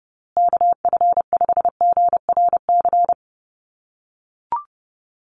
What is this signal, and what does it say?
Morse code, heard as a keyed tone, about 30 wpm, 700 Hz. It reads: KF5GRC
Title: USA Call Signs Repeated 2x 30wpm